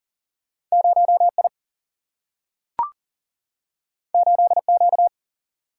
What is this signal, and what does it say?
Morse code, heard as a keyed tone, about 41 wpm, 700 Hz. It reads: 0I 8Q
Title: Single-Letter Single-Number 40wpm